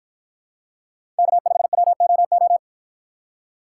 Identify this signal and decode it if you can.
Morse code, heard as a keyed tone, about 53 wpm, 700 Hz. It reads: K5WOO